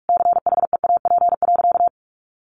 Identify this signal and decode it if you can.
Morse code, heard as a keyed tone, about 44 wpm, 700 Hz. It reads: CHEAP.
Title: Sentences from Top 1200 Words 45wpm